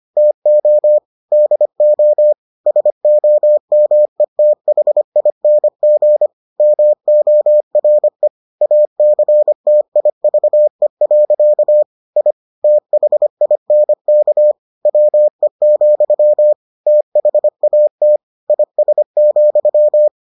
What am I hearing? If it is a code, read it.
Morse code, heard as a keyed tone, about 25 wpm, 600 Hz. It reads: TO DO SOMETHING MORE ACTIVE. I THINK WE, THAT IS,